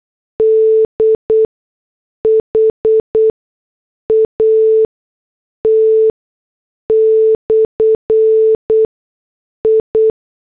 Morse code, heard as a keyed tone, about 8 wpm, spaced as for 6 wpm, 440 Hz. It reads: DHAT/I